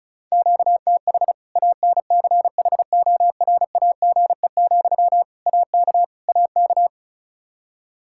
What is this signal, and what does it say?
Morse code, heard as a keyed tone, about 36 wpm, 700 Hz. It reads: QTH ANCHORAGE, AK AK
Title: Practice QSO - Exchange 1 - Beginning - 35wpm